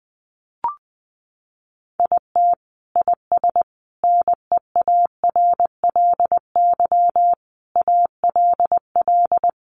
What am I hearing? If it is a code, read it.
Morse code, heard as a keyed tone, about 20 wpm, 700 Hz. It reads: IT IS NEARLY ALL
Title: Sentences from Top 1300 Words 20wpm